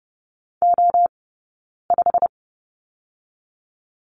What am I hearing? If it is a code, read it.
Morse code, heard as a keyed tone, about 30 wpm, 700 Hz. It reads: O 5
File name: course / 30:10wpm / 086 Lesson 15d - Number 5 - Two Characters 30wpm